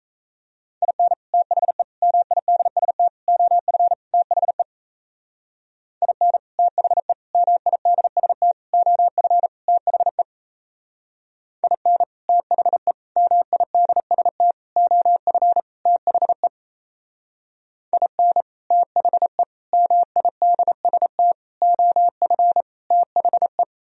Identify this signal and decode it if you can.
Morse code, heard as a keyed tone, about 34 wpm, 700 Hz. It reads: IN THE MIDST OF THE IN THE MIDST OF THE IN THE MIDST OF THE IN THE MIDST OF THE
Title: Speed-Racing - Sets of 5 Words 28wpm